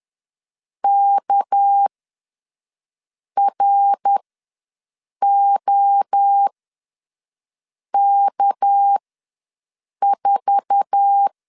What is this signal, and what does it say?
Morse code, heard as a keyed tone, about 10 wpm, spaced as for 5 wpm, 800 Hz. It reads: KROK4
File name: LEZIONI / 8 YJ45 / 8 YJ4 ws 10wpm 4020